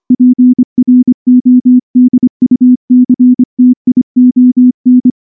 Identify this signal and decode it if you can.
Morse code, heard as a keyed tone, about 24 wpm, 250 Hz. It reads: PRODUCTION
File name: morse-training-25WPM